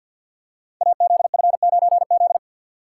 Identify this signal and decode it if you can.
Morse code, heard as a keyed tone, about 51 wpm, 700 Hz. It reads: AZF9Z
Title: Code Groups - 5 Letter:Numbers - 50wpm